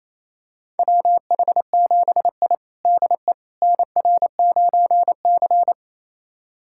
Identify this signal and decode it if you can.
Morse code, heard as a keyed tone, about 28 wpm, 700 Hz. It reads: WH7I DE NR9C
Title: USA Callsign DE Callsign - 4 Characters - 28wpm